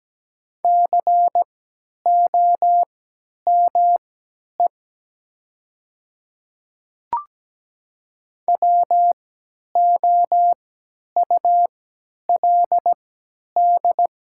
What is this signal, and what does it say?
Morse code, heard as a keyed tone, about 17 wpm, spaced as for 10 wpm, 700 Hz. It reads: COME WOULD
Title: 3x Character Spacing - Top 100 Words - 17wpm